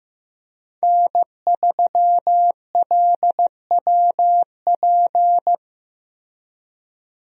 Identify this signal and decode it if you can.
Morse code, heard as a keyed tone, about 15 wpm, 700 Hz. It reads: N3LWP
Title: USA Call Signs - 5 Characters - Repeated 2x - 15wpm